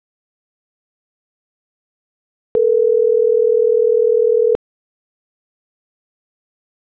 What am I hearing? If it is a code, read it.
Ringback tone (North America), 440 + 480 Hz.